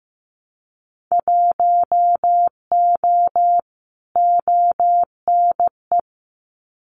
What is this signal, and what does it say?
Morse code, heard as a keyed tone, about 15 wpm, 700 Hz. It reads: 1O ONE
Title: Winter Field Day Exchanges 15wpm